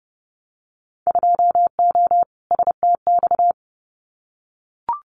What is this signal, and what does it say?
Morse code, heard as a keyed tone, about 30 wpm, 700 Hz. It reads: 2O STX
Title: Winter Field Day Exchanges 30wpm